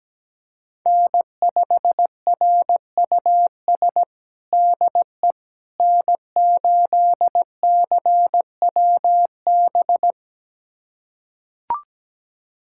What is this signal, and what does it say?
Morse code, heard as a keyed tone, about 17 wpm, 700 Hz. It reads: N5RUS DE N8CWB
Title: USA Callsign DE Callsign - 5 Characters - 17wpm